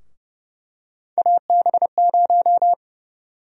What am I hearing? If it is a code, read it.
Morse code, heard as a keyed tone, about 30 wpm, 700 Hz. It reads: AB0
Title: USA Call Sign Prefixes - Repeated 2x - 30wpm